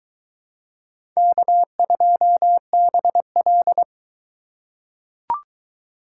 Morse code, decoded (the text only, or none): K2BL